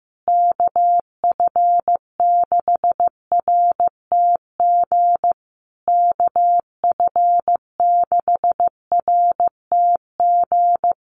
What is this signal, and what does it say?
Morse code, heard as a keyed tone, about 15 wpm, 700 Hz. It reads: KF6RTG KF6RTG
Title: USA Call Signs Repeated 3x 15wpm